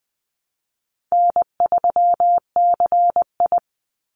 Morse code, heard as a keyed tone, about 20 wpm, 700 Hz. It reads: N3CI
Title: Easy Call Signs Repeated 2x - 20wpm